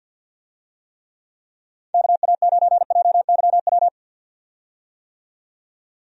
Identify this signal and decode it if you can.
Morse code, heard as a keyed tone, about 52 wpm, 700 Hz. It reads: KA9JYW
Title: USA Call Signs - 6 Characters - 50wpm